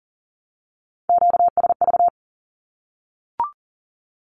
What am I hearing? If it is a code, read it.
Morse code, heard as a keyed tone, about 40 wpm, 700 Hz. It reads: QSV
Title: Q-Codes Repeated 3x 40wpm